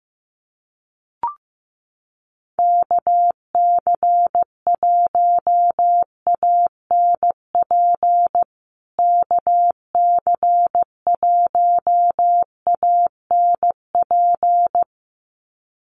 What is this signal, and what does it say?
Morse code, heard as a keyed tone, about 15 wpm, 700 Hz. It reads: KC1ANP KC1ANP
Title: USA Call Signs Repeated 2x 15wpm